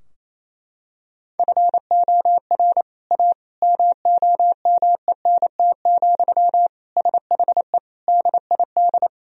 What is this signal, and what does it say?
Morse code, heard as a keyed tone, about 27 wpm, 700 Hz. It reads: FOR A MOMENT, SHE DID
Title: Sentences from Top 1000 Words - Review Part 2 - 28wpm